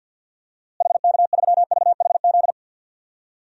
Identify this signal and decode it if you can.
Morse code, heard as a keyed tone, about 50 wpm, 700 Hz. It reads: HK3VHZ
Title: DXCC-100 Call Signs - 6 Characters - Repeated 2x - 50wpm